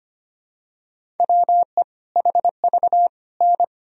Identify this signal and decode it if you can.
Morse code, heard as a keyed tone, about 25 wpm, 700 Hz. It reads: WE HV N
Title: Sentences from Top 100 Words and Abbreviations 25wpm